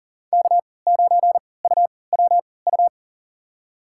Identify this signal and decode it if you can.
Morse code, heard as a keyed tone, about 40 wpm, spaced as for 23 wpm, 700 Hz. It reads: K9UWU
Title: USA Call Signs - 5 Characters - 3x Character Spacing - 40wpm